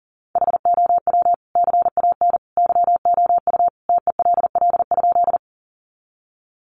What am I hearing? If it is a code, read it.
Morse code, heard as a keyed tone, about 40 wpm, 700 Hz. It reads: HOW CAN YOU TELL?
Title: Sentences from Top 200 Words 40wpm